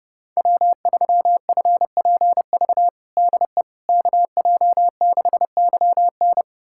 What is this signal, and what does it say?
Morse code, heard as a keyed tone, about 31 wpm, 700 Hz. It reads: W3FPV DE KJ6YN
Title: USA Callsign DE Callsign - 5 Characters - 30wpm